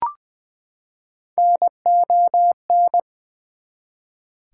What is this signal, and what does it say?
Morse code, heard as a keyed tone, about 21 wpm, 700 Hz. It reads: NON
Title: Prefixes and Suffixes 20wpm